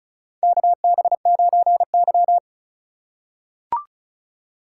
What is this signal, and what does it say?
Morse code, heard as a keyed tone, about 36 wpm, 700 Hz. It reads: KB9Y
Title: USA Call Signs - 4 Characters - 4x Character Spacing - 35wpm